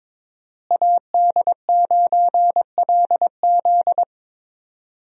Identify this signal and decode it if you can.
Morse code, heard as a keyed tone, about 22 wpm, 700 Hz. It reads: AD9LZ
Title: USA Call Signs - 5 Characters - Repeated 3x - 22wpm